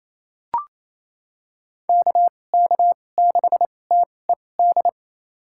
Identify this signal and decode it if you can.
Morse code, heard as a keyed tone, about 28 wpm, spaced as for 20 wpm, 700 Hz. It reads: KK6TED
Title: USA Call Signs - 6 Characters - 2x Character Spacing - 28wpm